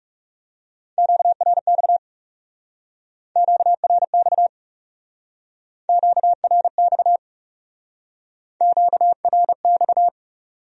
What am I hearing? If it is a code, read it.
Morse code, heard as a keyed tone, about 38 wpm, 700 Hz. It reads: QRX QRX QRX QRX